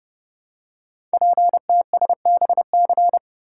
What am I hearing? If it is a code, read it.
Morse code, heard as a keyed tone, about 28 wpm, 700 Hz. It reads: PTSBC
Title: Code Groups - 5 Letters - 30wpm